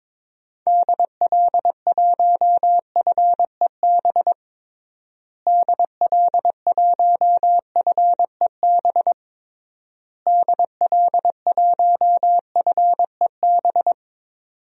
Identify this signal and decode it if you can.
Morse code, heard as a keyed tone, about 22 wpm, 700 Hz. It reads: DL1FEB DL1FEB DL1FEB
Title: DXCC-100 Call Signs - 6 Characters - Repeated 3x - 22wpm